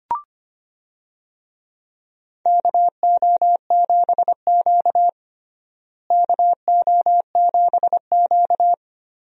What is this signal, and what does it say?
Morse code, heard as a keyed tone, about 25 wpm, 700 Hz. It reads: KO7Q KO7Q
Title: USA Call Signs - 4 Characters - Repeated 2x - 25wpm